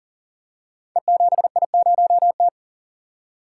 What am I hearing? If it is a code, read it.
Morse code, heard as a keyed tone, about 42 wpm, 700 Hz. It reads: E7I0T
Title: Code Groups - 5 Letter:Numbers - 40wpm